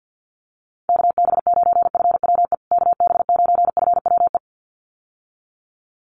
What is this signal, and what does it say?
Morse code, heard as a keyed tone, about 50 wpm, 700 Hz. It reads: KB9FWE KB9FWE
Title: USA Call Signs Repeated 2x 50wpm